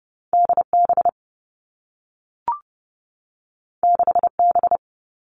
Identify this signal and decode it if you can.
Morse code, heard as a keyed tone, about 30 wpm, 700 Hz. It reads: D B 6 B
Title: Two Challenging Letter-Number Combinations 30wpm